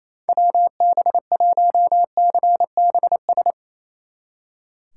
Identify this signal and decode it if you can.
Morse code, heard as a keyed tone, about 33 wpm, 700 Hz. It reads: WB1CBS
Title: Speed Racing - USA Call Signs 28wpm